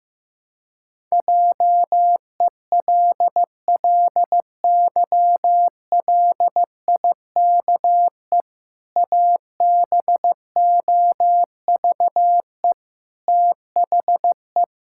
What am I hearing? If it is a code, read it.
Morse code, heard as a keyed tone, about 15 wpm, 700 Hz. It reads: JELLYLIKE ABOVE THE
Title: Pangram Challenge v2 - 15wpm